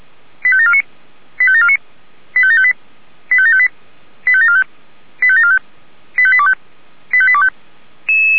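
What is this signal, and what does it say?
Five-tone selective call (ZVEI), heard as quick runs of five tones, tones of about 70 ms. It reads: 85749 85749 85758 85758 85745 85745 85825 85825
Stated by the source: ZVEI five-tone call 85727 received by SDR.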